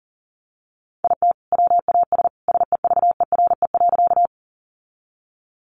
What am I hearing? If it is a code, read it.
Morse code, heard as a keyed tone, about 40 wpm, 700 Hz. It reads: IT WAS SEVERE.
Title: Sentences from Top 1400 Words 40wpm